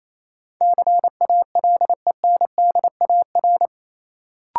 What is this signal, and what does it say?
Morse code, heard as a keyed tone, about 28 wpm, 700 Hz. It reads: CALENDAR
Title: Sentences from Top 1600 Words - New Word Drill Alt - 28wpm